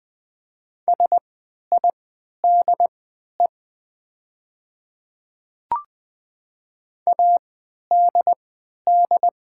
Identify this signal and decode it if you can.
Morse code, heard as a keyed tone, about 20 wpm, spaced as for 11 wpm, 700 Hz. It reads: SIDE ADD
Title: 3x Character Spacing - Top 200 Words - 20wpm